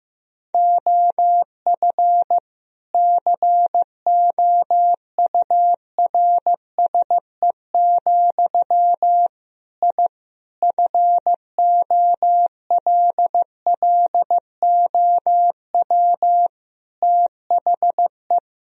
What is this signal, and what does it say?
Morse code, heard as a keyed tone, about 15 wpm, 700 Hz. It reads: OF COURSE, I FOLLOW THE
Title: Sentences from Top 1000 Words - Review Part 4 - 15wpm